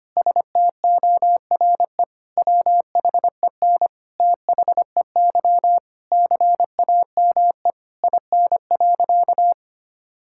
Morse code, heard as a keyed tone, about 25 wpm, 700 Hz. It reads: STORE WHEN THEY CAME IN.
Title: Sentences from Top 1000 Words - Review Part 5 - 25wpm